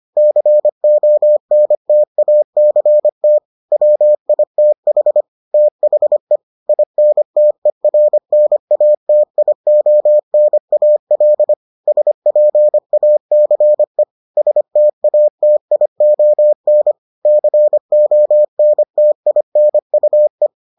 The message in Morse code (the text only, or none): CONTACT WITH THE INTERNATIONAL SPACE STATION CONTINUE